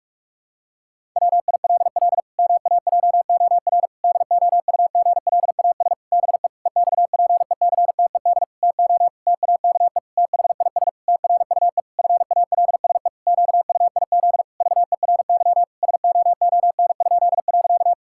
Morse code, heard as a keyed tone, about 47 wpm, 700 Hz. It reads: WILL MAJOR DOUGLAS BE EXPECTED TO TAKE THIS TRUE FALSE QUIZ VERY SOON?.
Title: Pangram Challenge v2 - 45wpm